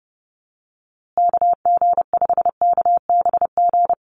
Morse code, heard as a keyed tone, about 30 wpm, 700 Hz. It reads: KG5KBG
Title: USA Call Signs Repeated 3x 30wpm